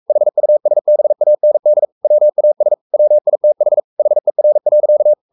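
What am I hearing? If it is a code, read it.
Morse code, heard as a keyed tone, about 44 wpm, 600 Hz. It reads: HUSBAND WAS WITH HER.
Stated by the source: Morse code at 43 wpm sending Ken Follett's novel Never in English, from CW med Gnister.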